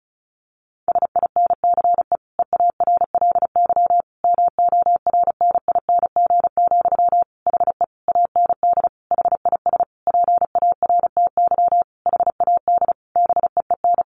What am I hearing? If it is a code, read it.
Morse code, heard as a keyed tone, about 34 wpm, 700 Hz. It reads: SINCE EARLY MORNING, HE AND HIS PARTY HAD BEEN
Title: Sentences from Top 1400 Words 35wpm